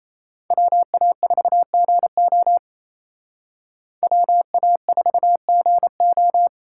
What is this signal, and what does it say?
Morse code, heard as a keyed tone, about 35 wpm, 700 Hz. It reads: WA4GO WA4GO
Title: Speed Racing - USA 2x2 Call Signs 28wpm